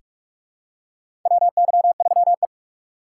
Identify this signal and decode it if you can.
Morse code, heard as a keyed tone, about 46 wpm, 700 Hz. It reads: WY3E
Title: USA Call Signs - 4 Characters - 2x Character Spacing - 45wpm